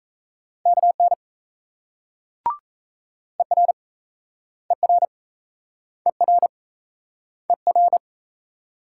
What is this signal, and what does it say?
Morse code, heard as a keyed tone, about 38 wpm, 700 Hz. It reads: KN ER ER ER ER